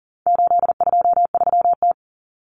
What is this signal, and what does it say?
Morse code, heard as a keyed tone, about 40 wpm, 700 Hz. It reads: 823T
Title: Common Cut Numbers - 4-Digit 40wpm